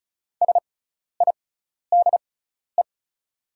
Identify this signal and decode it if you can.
Morse code, heard as a keyed tone, about 34 wpm, spaced as for 12 wpm, 700 Hz. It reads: SIDE
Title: 6x Character Spacing - Top 300 Words - 35wpm